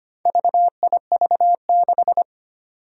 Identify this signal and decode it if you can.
Morse code, heard as a keyed tone, about 25 wpm, 700 Hz. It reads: VIV6
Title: Four Challenging Letter-Number Combinations 25wpm